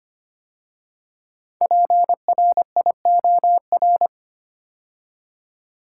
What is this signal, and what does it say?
Morse code, heard as a keyed tone, about 25 wpm, 700 Hz. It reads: PRIOR
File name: 5 Letter Words - First 3 Letters Given - 25wpm